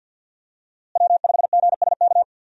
Morse code, heard as a keyed tone, about 50 wpm, 700 Hz. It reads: W5GSK
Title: USA Call Signs - 5 Characters - Repeated 2x - 50wpm